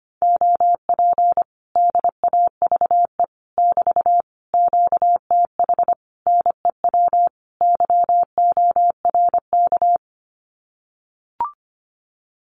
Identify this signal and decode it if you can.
Morse code, heard as a keyed tone, about 25 wpm, 700 Hz. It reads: OP DAVE <BT> QTH NEW YORK
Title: Common QSO Phrases Encore 25wpm